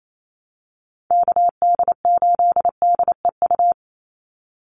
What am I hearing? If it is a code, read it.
Morse code, heard as a keyed tone, about 28 wpm, 700 Hz. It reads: KD8DEU
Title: NA SOTA Chaser Callsigns 28wpm